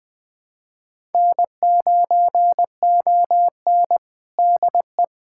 Morse code, heard as a keyed tone, about 20 wpm, 700 Hz. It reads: N9ON DE
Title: USA Callsign DE Callsign - 4 Characters - 20wpm